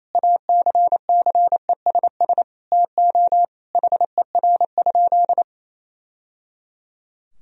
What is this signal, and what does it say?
Morse code, heard as a keyed tone, about 28 wpm, 700 Hz. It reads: ACCESS TO HER?
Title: Sentences from Top 1000 Words - Review Part 4 - 28wpm